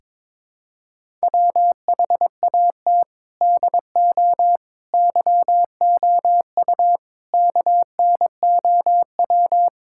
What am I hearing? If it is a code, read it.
Morse code, heard as a keyed tone, about 22 wpm, 700 Hz. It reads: WHAT DO YOU KNOW
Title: Sentences from Top 1000 Words - Review Part 5 - 22wpm